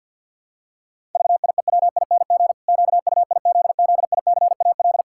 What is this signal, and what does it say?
Morse code, heard as a keyed tone, about 49 wpm, 700 Hz. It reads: VIEWING QUIZZICAL
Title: Pangram Challenge v2 - 50wpm